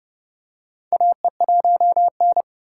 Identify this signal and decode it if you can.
Morse code, heard as a keyed tone, about 29 wpm, 700 Hz. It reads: AE1N
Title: K1USN SST QSO Practice - 30wpm